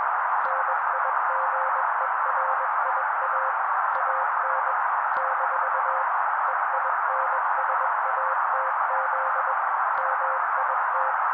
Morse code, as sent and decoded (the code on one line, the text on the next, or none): -. .. --. .-. .. .- / .- -. / -....- / . .. -. ... .- - --.. / -- .. -
NIGERIA AN - EINSATZ MIT